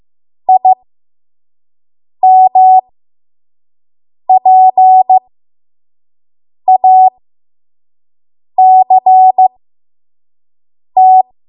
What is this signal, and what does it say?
Morse code, heard as a keyed tone, about 15 wpm, spaced as for 5 wpm, 750 Hz. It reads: IMPACT
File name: Morse / 05WPM